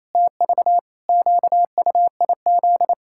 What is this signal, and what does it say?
Morse code, heard as a keyed tone, about 28 wpm, 700 Hz. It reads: TV QUIZ
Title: Pangram Challenge v2 - 28wpm